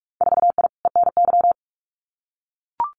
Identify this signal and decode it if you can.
Morse code, heard as a keyed tone, about 44 wpm, 700 Hz. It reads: 4I ENY
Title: Winter Field Day Exchanges 45wpm